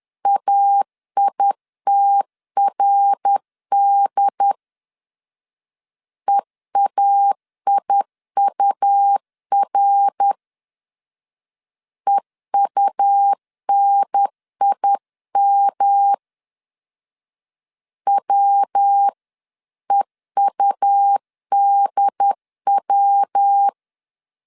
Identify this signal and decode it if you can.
Morse code, heard as a keyed tone, about 10 wpm, 800 Hz. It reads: AITRD EAIUR EUNIM WEUDW
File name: CW / DET / 1 3 1500C S C 10wpm 3335